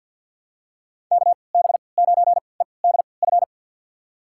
Morse code, heard as a keyed tone, about 51 wpm, spaced as for 28 wpm, 700 Hz. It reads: KB9EDF
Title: USA Call Signs - 6 Characters - 3x Character Spacing - 50wpm